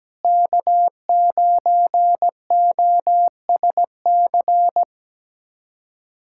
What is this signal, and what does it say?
Morse code, heard as a keyed tone, about 17 wpm, 700 Hz. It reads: K9OSC
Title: USA Callsign DE Callsign - 5 Characters - 17wpm